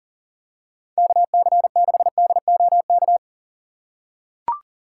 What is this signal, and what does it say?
Morse code, heard as a keyed tone, about 37 wpm, 700 Hz. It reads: KC6DOK
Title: USA Call Signs - 6 Characters - 2x Character Spacing - 40wpm